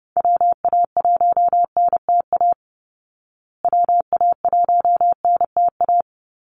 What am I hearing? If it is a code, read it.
Morse code, heard as a keyed tone, about 30 wpm, 700 Hz. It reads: WA1NTA WA1NTA
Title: Call Signs Repeated 2x - 30wpm